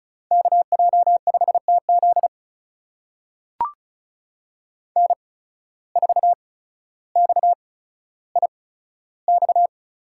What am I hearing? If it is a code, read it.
Morse code, heard as a keyed tone, about 36 wpm, spaced as for 15 wpm, 690 Hz. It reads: KJ5TZ N4XIX
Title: USA Call Signs - 5 Characters - 8x Character Spacing - 35wpm